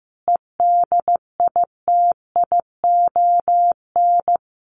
Morse code, heard as a keyed tone, about 15 wpm, 700 Hz. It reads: EDITION
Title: Sets of 5 Words 15wpm